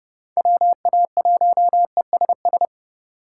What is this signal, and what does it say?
Morse code, heard as a keyed tone, about 31 wpm, 700 Hz. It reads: WA1ESS
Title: USA Call Signs - 3x Character Spacing - 30wpm